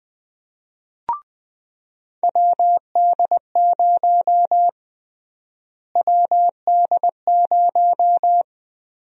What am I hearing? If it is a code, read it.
Morse code, heard as a keyed tone, about 20 wpm, 700 Hz. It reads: WD0 WD0
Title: USA Call Sign Prefixes - Repeated 2x - 20wpm